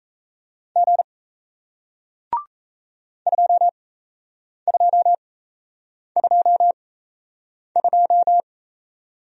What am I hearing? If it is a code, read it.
Morse code, heard as a keyed tone, about 38 wpm, spaced as for 8 wpm, 700 Hz. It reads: G 2222